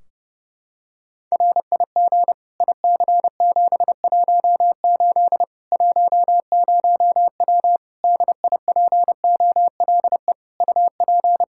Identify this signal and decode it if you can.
Morse code, heard as a keyed tone, about 32 wpm, 700 Hz. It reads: RIG IC718 10W DIPOLE UP
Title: Practice QSO - Exchange 2 - Info - 30wpm